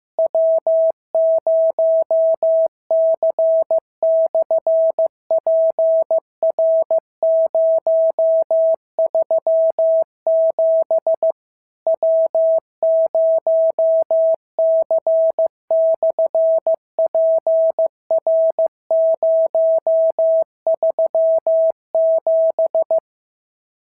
Morse code, heard as a keyed tone, about 15 wpm, 650 Hz. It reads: W0C/PR037 W0C/PR037
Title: SOTA QSO Practice 15wpm